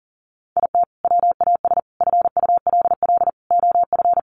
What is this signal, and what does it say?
Morse code, heard as a keyed tone, about 40 wpm, 700 Hz. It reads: IT WAS FULL OF
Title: Sentences from Top 500 Words 40wpm